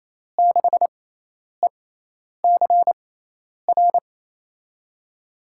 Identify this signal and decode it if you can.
Morse code, heard as a keyed tone, about 28 wpm, spaced as for 10 wpm, 700 Hz. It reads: 6ECR
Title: USA Call Signs - 6 Characters - 6x Character Spacing - 28wpm